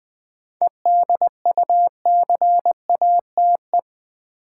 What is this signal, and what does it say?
Morse code, heard as a keyed tone, about 20 wpm, 700 Hz. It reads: EDUCATE
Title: 7 Letter Words - First 3 Letters Given - 20wpm